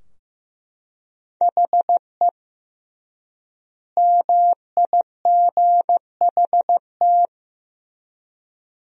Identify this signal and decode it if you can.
Morse code, heard as a keyed tone, about 15 wpm, 700 Hz. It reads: HE MIGHT
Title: 3x Word-Spacing - Sentences from Top 300 Words - 15wpm